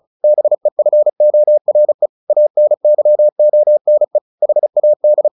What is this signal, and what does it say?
Morse code, heard as a keyed tone, about 34 wpm, 600 Hz. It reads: BEFORE ANYONE HAD
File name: the old man and the sea 2 35wpm